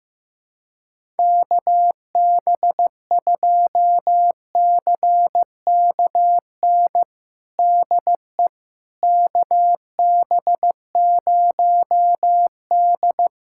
Morse code, heard as a keyed tone, about 15 wpm, 700 Hz. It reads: KB2CKN DE KB0D
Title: USA Callsign DE Callsign 15wpm